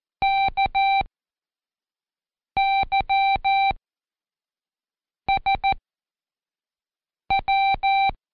Morse code, heard as a keyed tone, about 14 wpm, spaced as for 5 wpm, 780 Hz. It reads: KYSW
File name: Code Quick / AudioDownld1 / Seg18-Day2-Quiz ASYO